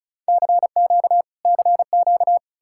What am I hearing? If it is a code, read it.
Morse code, heard as a keyed tone, about 46 wpm, spaced as for 41 wpm, 700 Hz. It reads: CQ CQ
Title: Speed-Racing Common QSO Phrases - 35wpm